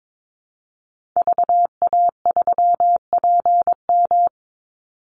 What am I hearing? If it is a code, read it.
Morse code, heard as a keyed tone, about 22 wpm, 700 Hz. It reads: VA3PM
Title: Call Signs - 22wpm